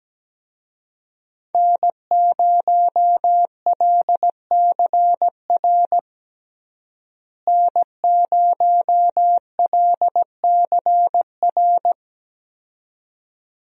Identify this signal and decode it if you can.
Morse code, heard as a keyed tone, about 17 wpm, 700 Hz. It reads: N0LCR N0LCR
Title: USA Call Signs - 5 Characters - Repeated 2x - 17wpm